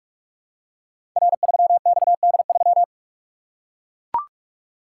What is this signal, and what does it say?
Morse code, heard as a keyed tone, about 44 wpm, 700 Hz. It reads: R3XD3
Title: Code Groups - 5 Letter:Numbers - 45wpm